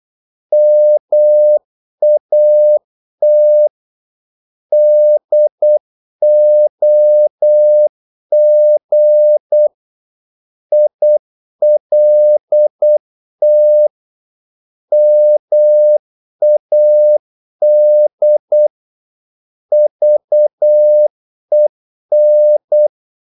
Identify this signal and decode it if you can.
Morse code, heard as a keyed tone, about 8 wpm, 600 Hz. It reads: MAT DOG ILT MAD VEN
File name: Korte ord DK 8wpm